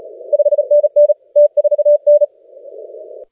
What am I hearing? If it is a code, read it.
Morse code, heard as a keyed tone, about 34 wpm, 590 Hz. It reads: 5NN T4N
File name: rec contest / TTC / SP4TKR de SN7O TTC-SPCWC CW 3,5MHz